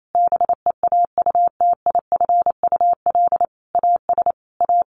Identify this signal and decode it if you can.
Morse code, heard as a keyed tone, about 28 wpm, 700 Hz. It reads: BEAUTIFUL AS A
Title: Sentences from Top 1500 Words 28wpm